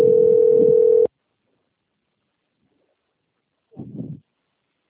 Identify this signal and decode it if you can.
Ringback tone (North America), 440 + 480 Hz.